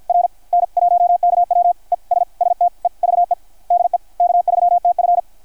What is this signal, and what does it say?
Morse code, heard as a keyed tone, about 53 wpm, spaced as for 46 wpm, 690 Hz. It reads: R N 1KW ES STEVE DE K3TF
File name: archived / CRSnet